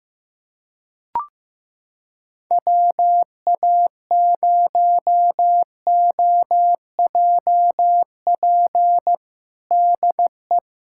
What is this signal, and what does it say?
Morse code, heard as a keyed tone, about 15 wpm, 700 Hz. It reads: WA0OJP DE
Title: USA Callsign DE Callsign - 6 Characters - 15wpm